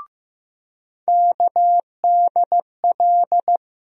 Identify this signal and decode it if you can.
Morse code, heard as a keyed tone, about 16 wpm, 700 Hz. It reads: KDL
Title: USA Call Sign Postfixes - 3 Letters 15wpm